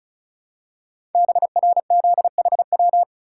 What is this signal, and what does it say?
Morse code, heard as a keyed tone, about 36 wpm, 700 Hz. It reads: BRZHW
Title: Code Groups - 5 Letters - 35wpm